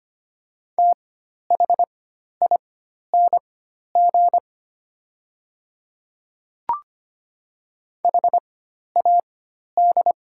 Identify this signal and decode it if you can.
Morse code, heard as a keyed tone, about 25 wpm, spaced as for 12 wpm, 700 Hz. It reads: THING HAD
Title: 4x Character Spacing - Top 100 Words - 25wpm